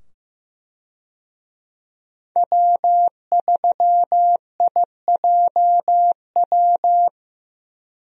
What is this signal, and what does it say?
Morse code, heard as a keyed tone, about 15 wpm, 700 Hz. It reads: W3IJW
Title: USA Call Signs - 5 Characters - 15wpm